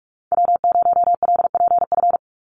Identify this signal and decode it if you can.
Morse code, heard as a keyed tone, about 44 wpm, 700 Hz. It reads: W0LPF
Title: Call Signs Repeated 3x - 45wpm